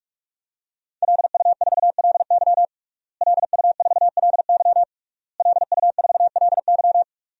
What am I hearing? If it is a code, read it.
Morse code, heard as a keyed tone, about 41 wpm, 700 Hz. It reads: LU4LY LU4LY LU4LY
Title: DXCC-100 Call Signs - 5 Characters - Repeated 3x - 45wpm